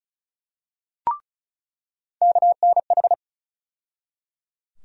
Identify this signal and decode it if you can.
Morse code, heard as a keyed tone, about 36 wpm, 700 Hz. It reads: KNH
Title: USA Call Sign Postfixes - 3 Letters 35wpm